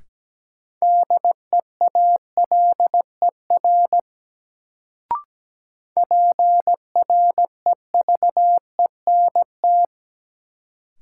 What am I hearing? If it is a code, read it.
Morse code, heard as a keyed tone, about 17 wpm, 700 Hz. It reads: DEALER PREVENT
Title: Sentences from Top 1100 Words - New Word Drill - 17wpm